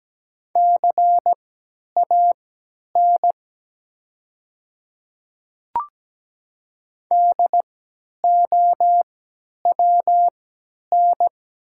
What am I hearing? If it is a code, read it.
Morse code, heard as a keyed tone, about 17 wpm, spaced as for 10 wpm, 700 Hz. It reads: CAN DOWN
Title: 3x Character Spacing - Top 500 Words - 17wpm